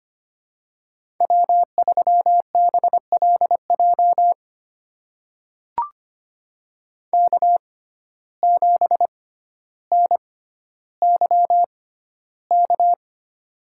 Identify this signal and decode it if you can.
Morse code, heard as a keyed tone, about 25 wpm, spaced as for 13 wpm, 700 Hz. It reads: W3BLJ K7NYK
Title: USA Call Signs - 5 Characters - 6x Character Spacing - 25wpm